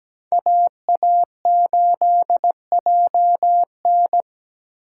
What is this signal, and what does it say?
Morse code, heard as a keyed tone, about 17 wpm, 700 Hz. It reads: AA8JN
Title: USA Call Signs - 5 Characters - 2x Character Spacing - 17wpm